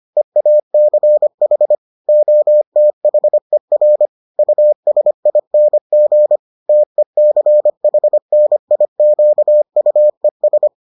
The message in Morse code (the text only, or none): EACH OTHER USING TECHNIQUES